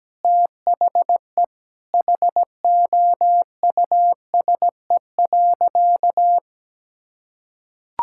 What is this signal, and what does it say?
Morse code, heard as a keyed tone, about 17 wpm, 700 Hz. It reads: THE HOUSE.
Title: Sentences from Top 1000 Words - Review Part 5 - 17wpm